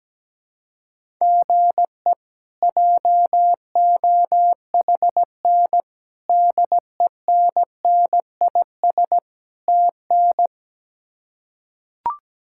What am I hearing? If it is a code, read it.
Morse code, heard as a keyed tone, about 17 wpm, 700 Hz. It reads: GE JOHN DENNIS TN
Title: K1USN SST QSO Practice - 17wpm